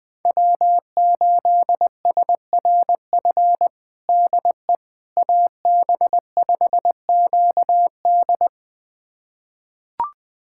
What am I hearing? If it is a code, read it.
Morse code, heard as a keyed tone, about 21 wpm, 700 Hz. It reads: W8SRF DE AB5QD
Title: USA Callsign DE Callsign - 5 Characters - 20wpm